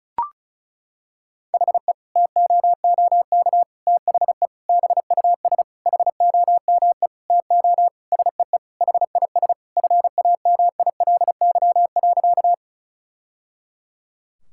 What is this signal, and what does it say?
Morse code, heard as a keyed tone, about 34 wpm, 700 Hz. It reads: HE TOOK THE BUS HOME TO SEE HIS FAMILY.
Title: Sentences from Top 1000 Words - Review Part 6 - 35wpm